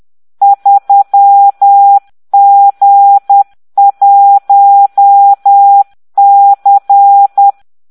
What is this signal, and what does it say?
Morse code, heard as a keyed tone, about 10 wpm, 800 Hz. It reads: 3G1C